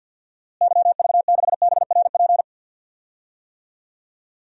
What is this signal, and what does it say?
Morse code, heard as a keyed tone, about 52 wpm, 700 Hz. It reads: YV6BRP
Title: DXCC-100 Call Signs - 6 Characters - Repeated 2x - 50wpm